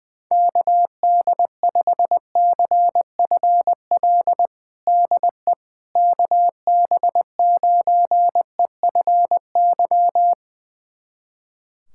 Morse code, heard as a keyed tone, about 21 wpm, 700 Hz. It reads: KD5CFL DE KB9EFY
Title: USA Callsign DE Callsign 20wpm